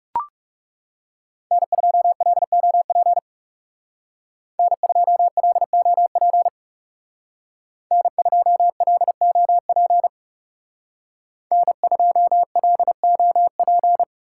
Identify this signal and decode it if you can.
Morse code, heard as a keyed tone, about 34 wpm, 700 Hz. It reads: N2LOP N2LOP N2LOP N2LOP
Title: Speed Racing - USA Call Signs 30wpm